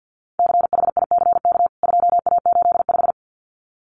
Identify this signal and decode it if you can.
Morse code, heard as a keyed tone, about 49 wpm, 700 Hz. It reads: CHICK 2A85
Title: CWT Exchange 50wpm